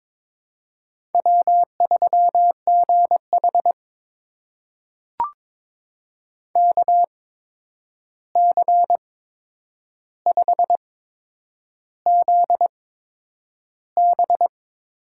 Morse code, heard as a keyed tone, about 21 wpm, spaced as for 9 wpm, 700 Hz. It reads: W3GH KC5ZB
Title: USA Call Signs - 8x Character Spacing - 22wpm